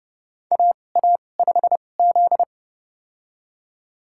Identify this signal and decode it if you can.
Morse code, heard as a keyed tone, about 30 wpm, spaced as for 22 wpm, 700 Hz. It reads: AA5Z
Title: USA Call Signs - 4 Characters - 2x Character Spacing - 30wpm